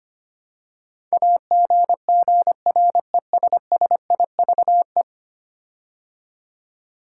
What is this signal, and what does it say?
Morse code, heard as a keyed tone, about 25 wpm, 700 Hz. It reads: AGGRESSIVE
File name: 10 Letter Words - First 3 Letters Given - 25wpm